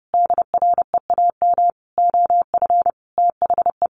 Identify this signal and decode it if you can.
Morse code, heard as a keyed tone, about 30 wpm, 700 Hz. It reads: DREAM OF THE
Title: Sentences from Top 1500 Words 30wpm